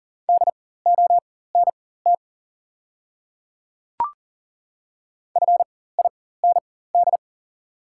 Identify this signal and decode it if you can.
Morse code, heard as a keyed tone, about 42 wpm, spaced as for 19 wpm, 690 Hz. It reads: DONT FIND
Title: 4x Character Spacing - Top 300 Words - 40wpm